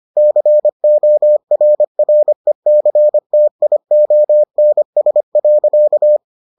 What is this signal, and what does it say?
Morse code, heard as a keyed tone, about 25 wpm, 600 Hz. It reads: CORRECTIONS.